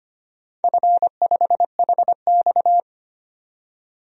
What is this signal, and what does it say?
Morse code, heard as a keyed tone, about 25 wpm, 700 Hz. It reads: F5HX
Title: DXCC-100 Call Signs - 4 Characters - 25wpm